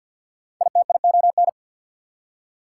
Morse code, heard as a keyed tone, about 69 wpm, spaced as for 59 wpm, 690 Hz. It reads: ITION
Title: Speed-Racing Prefixes and Suffixes 50wpm